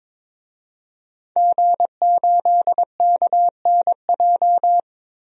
Morse code, heard as a keyed tone, about 22 wpm, 700 Hz. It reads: G8KNJ
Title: DXCC-100 Call Signs - 5 Characters - 22wpm